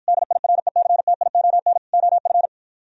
Morse code, heard as a keyed tone, about 52 wpm, 700 Hz. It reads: DIRECTION OF